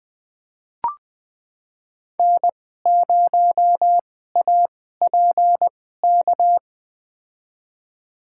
Morse code, heard as a keyed tone, about 20 wpm, spaced as for 15 wpm, 700 Hz. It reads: N0APK
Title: USA Call Signs - 5 Characters - 2x Character Spacing - 20wpm